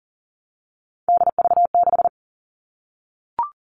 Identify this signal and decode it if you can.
Morse code, heard as a keyed tone, about 40 wpm, 700 Hz. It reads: DV6
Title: All Cut Numbers - 3-Digit 40wpm